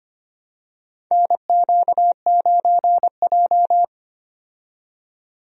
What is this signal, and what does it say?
Morse code, heard as a keyed tone, about 25 wpm, 700 Hz. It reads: NQ9J
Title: USA Call Signs - 4 Characters - 25wpm